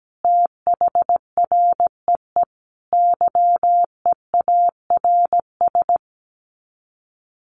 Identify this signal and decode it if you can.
Morse code, heard as a keyed tone, about 17 wpm, 700 Hz. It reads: THREE YEARS
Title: Sets of 5 Words Encore2 17wpm